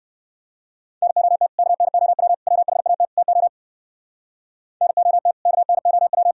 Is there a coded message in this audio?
Yes: Morse code from 1.0 to 6.3 s, heard as a keyed tone, about 56 wpm, 700 Hz. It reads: NOT KNOW WHAT TO NOT KNOW